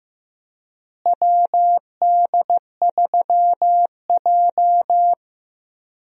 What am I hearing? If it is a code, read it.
Morse code, heard as a keyed tone, about 15 wpm, 700 Hz. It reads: WD3J
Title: USA Call Signs - 4 Characters - Repeated 2x - 15wpm